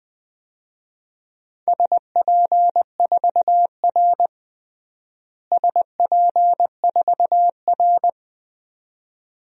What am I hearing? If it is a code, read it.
Morse code, heard as a keyed tone, about 20 wpm, 700 Hz. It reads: SP4R SP4R
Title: DXCC-100 Call Signs - 4 Characters - Repeated 2x - 20wpm